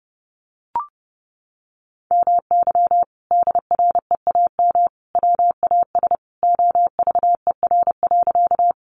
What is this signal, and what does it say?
Morse code, heard as a keyed tone, about 30 wpm, 700 Hz. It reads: MY DREAM WAS OVER.
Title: Sentences from Top 1500 Words 30wpm